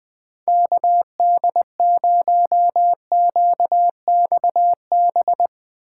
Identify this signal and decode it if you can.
Morse code, heard as a keyed tone, about 21 wpm, 700 Hz. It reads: KD0QXB